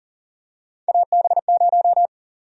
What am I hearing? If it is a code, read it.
Morse code, heard as a keyed tone, about 40 wpm, 700 Hz. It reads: AB0
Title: USA Call Sign Prefixes - 40wpm